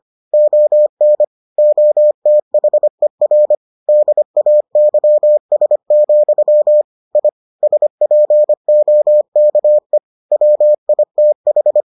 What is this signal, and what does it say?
Morse code, heard as a keyed tone, about 25 wpm, 600 Hz. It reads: ON OTHER DAYS, I SPOKE WITH